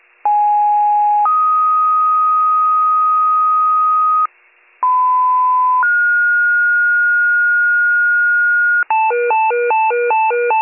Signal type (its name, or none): two-tone page